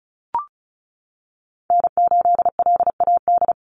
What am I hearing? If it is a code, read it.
Morse code, heard as a keyed tone, about 34 wpm, 700 Hz. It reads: N8LAD
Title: USA Call Signs 35wpm